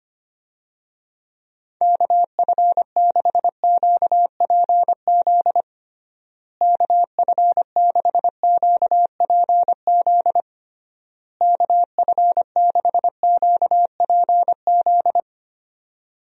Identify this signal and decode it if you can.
Morse code, heard as a keyed tone, about 25 wpm, 700 Hz. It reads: KF6QPZ KF6QPZ KF6QPZ
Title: USA Call Signs - 6 Characters - Repeated 3x - 25wpm